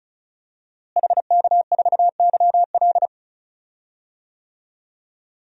Morse code, heard as a keyed tone, about 36 wpm, 700 Hz. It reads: HK4YL
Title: DXCC-100 Call Signs - 5 Characters - Repeated 3x - 35wpm